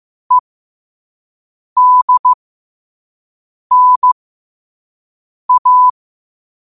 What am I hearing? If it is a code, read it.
Morse code, heard as a keyed tone, about 15 wpm, spaced as for 5 wpm, 1000 Hz. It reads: EDNA
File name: qotdfiles / 5wpm